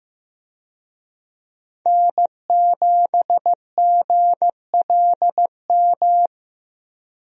Morse code, heard as a keyed tone, about 15 wpm, 700 Hz. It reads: N7GLM
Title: USA Call Signs - 5 Characters - Repeated 2x - 15wpm